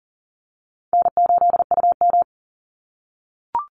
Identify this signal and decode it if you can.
Morse code, heard as a keyed tone, about 38 wpm, 700 Hz. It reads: N8UM
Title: Easy Call Signs - 40wpm